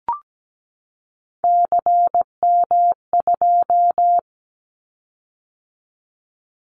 Morse code, heard as a keyed tone, about 17 wpm, 700 Hz. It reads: CM2
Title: Callsign Prefixes for Top 100 DXCC - 17wpm